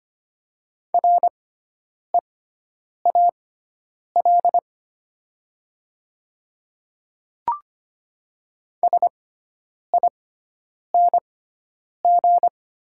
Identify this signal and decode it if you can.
Morse code, heard as a keyed tone, about 25 wpm, spaced as for 9 wpm, 700 Hz. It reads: REAL SING
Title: 6x Character Spacing - Top 500 Words - 25wpm